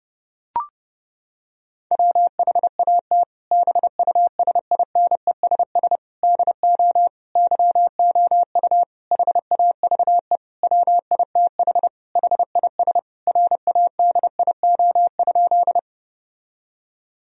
Morse code, heard as a keyed tone, about 30 wpm, 700 Hz. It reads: WHAT BUSINESS DO YOU HAVE WITH HIS RADIO?
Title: Sentences from Top 1000 Words - Review Part 3 - 30wpm